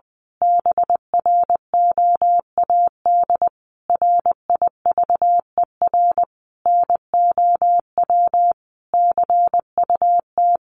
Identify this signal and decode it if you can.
Morse code, heard as a keyed tone, about 20 wpm, 700 Hz. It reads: BROAD RIVER NOW CUT
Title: Sentences from Top 1000 Words 20wpm